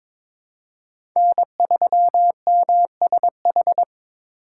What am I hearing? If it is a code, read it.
Morse code, heard as a keyed tone, about 22 wpm, 700 Hz. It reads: N3MSH
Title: USA Call Signs - 5 Characters - 4x Character Spacing - 22wpm